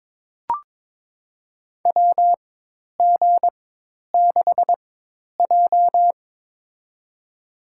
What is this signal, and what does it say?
Morse code, heard as a keyed tone, about 22 wpm, spaced as for 10 wpm, 700 Hz. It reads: WG6J
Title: USA Call Signs - 4x Character Spacing - 22wpm